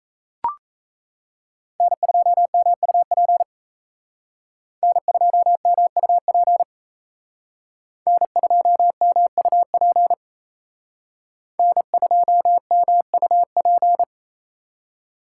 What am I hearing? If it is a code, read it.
Morse code, heard as a keyed tone, about 36 wpm, 700 Hz. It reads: N2MUP N2MUP N2MUP N2MUP